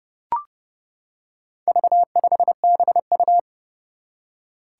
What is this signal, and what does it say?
Morse code, heard as a keyed tone, about 30 wpm, 700 Hz. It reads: V5BU
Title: Four Challenging Letter-Number Combinations 30wpm